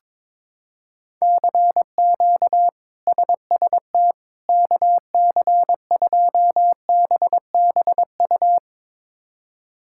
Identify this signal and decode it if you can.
Morse code, heard as a keyed tone, about 22 wpm, 700 Hz. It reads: CQ SST KC2BBU
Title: K1USN SST QSO Practice - 22wpm